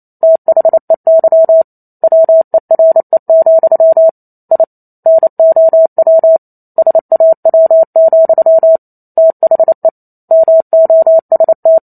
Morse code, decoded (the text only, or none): THEY WERE, I NOW SAW, THE MOST